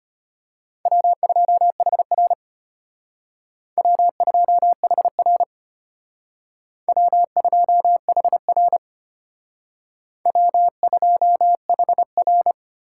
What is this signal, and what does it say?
Morse code, heard as a keyed tone, about 32 wpm, 700 Hz. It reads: W2HR W2HR W2HR W2HR